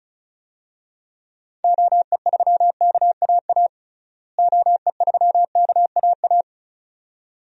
Morse code, heard as a keyed tone, about 34 wpm, 700 Hz. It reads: OE3KAA OE3KAA
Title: DXCC-100 Call Signs - 6 Characters - Repeated 2x - 35wpm